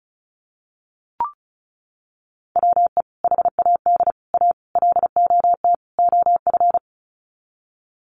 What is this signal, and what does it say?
Morse code, heard as a keyed tone, about 34 wpm, 700 Hz. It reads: WE HAD A LOT OF
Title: Sets of 5 Words Encore 35wpm